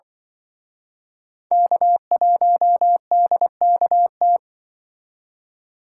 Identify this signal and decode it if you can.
Morse code, heard as a keyed tone, about 23 wpm, 700 Hz. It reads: K1DKT
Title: Speed Racing - USA Call Signs 20wpm